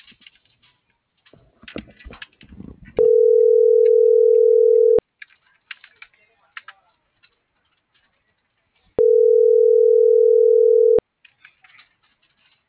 Ringback tone (North America), 440 + 480 Hz.
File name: CENTRAL MULTIMODAL / out francisco.cobos - .12806